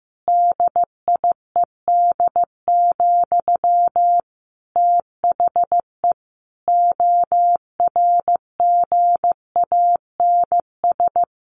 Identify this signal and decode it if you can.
Morse code, heard as a keyed tone, about 15 wpm, 700 Hz. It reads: DIED, THE ORGANS
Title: A Sound Of Thunder 15wpm